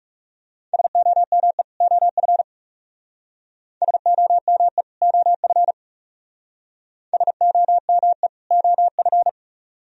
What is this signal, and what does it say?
Morse code, heard as a keyed tone, about 35 wpm, 700 Hz. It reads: SOME OF SOME OF SOME OF
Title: Speed-Racing - Sets of 2 Words 30wpm